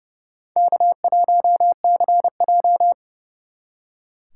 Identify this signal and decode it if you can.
Morse code, heard as a keyed tone, about 39 wpm, 700 Hz. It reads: K1CJ